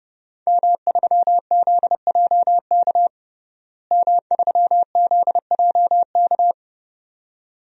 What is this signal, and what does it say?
Morse code, heard as a keyed tone, about 31 wpm, 700 Hz. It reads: M3ZJK M3ZJK
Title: DXCC-100 Call Signs - 5 Characters - Repeated 2x - 30wpm